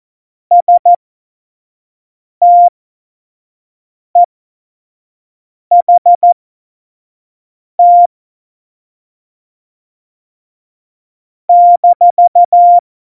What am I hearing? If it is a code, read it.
Morse code, heard as a keyed tone, about 14 wpm, spaced as for 5 wpm, 700 Hz. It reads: STEHT -
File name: Morsekurs / Witze / Witz 18 Juli 17, Tempo 25 F70